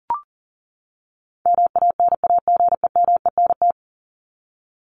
Morse code, heard as a keyed tone, about 40 wpm, 700 Hz. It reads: MANAGEMENT
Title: Top 501-750 Words 40wpm